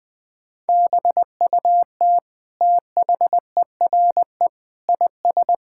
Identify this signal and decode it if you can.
Morse code, heard as a keyed tone, about 20 wpm, 700 Hz. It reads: BUT THERE IS
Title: Sets of 9 Words - 20wpm